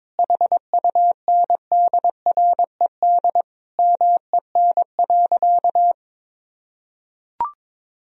Morse code, heard as a keyed tone, about 22 wpm, 700 Hz. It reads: HUNDRED MEN.
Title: Sentences from Top 1000 Words - Review Part 2 - 22wpm